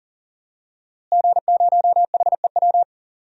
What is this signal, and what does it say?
Morse code, heard as a keyed tone, about 40 wpm, 700 Hz. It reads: G0HEW
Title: DXCC-100 Call Signs - 5 Characters - Repeated 3x - 40wpm